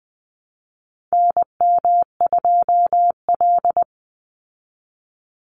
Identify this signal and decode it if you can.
Morse code, heard as a keyed tone, about 20 wpm, 700 Hz. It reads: NM2L
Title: Call Signs Repeated 3x - 20wpm